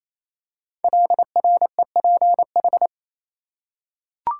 Morse code, heard as a keyed tone, about 27 wpm, 700 Hz. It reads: LREPH
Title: Code Groups - 5 Letters - 28wpm